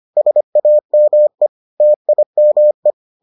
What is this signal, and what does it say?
Morse code, heard as a keyed tone, about 25 wpm, 600 Hz. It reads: SAME TIME